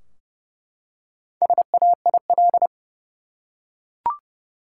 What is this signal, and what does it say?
Morse code, heard as a keyed tone, about 30 wpm, 700 Hz. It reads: SAIL
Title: Four Letter Words - 30wpm